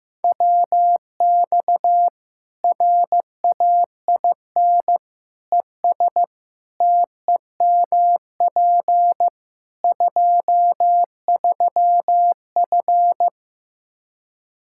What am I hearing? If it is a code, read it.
Morse code, heard as a keyed tone, about 15 wpm, 700 Hz. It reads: WX RAIN ES TEMP 23F
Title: Practice QSO - Exchange 2 - Info - 15wpm